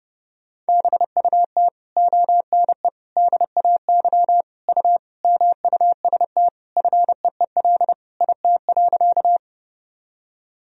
Morse code, heard as a keyed tone, about 32 wpm, 700 Hz. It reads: BUT ONE DAY U MUST FEEL IT.
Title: Sentences from Top 400 Words and Abbreviations 30wpm